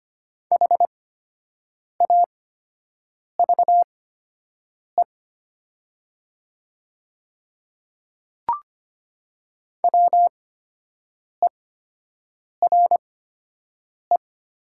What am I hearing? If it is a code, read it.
Morse code, heard as a keyed tone, about 25 wpm, spaced as for 7 wpm, 700 Hz. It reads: HAVE WERE
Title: 8x Character Spacing - Top 200 Words - 25wpm